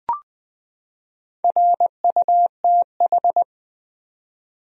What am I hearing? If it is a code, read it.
Morse code, heard as a keyed tone, about 20 wpm, 700 Hz. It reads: RUTH
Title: USA Common Names - Max 4 Letters - 20wpm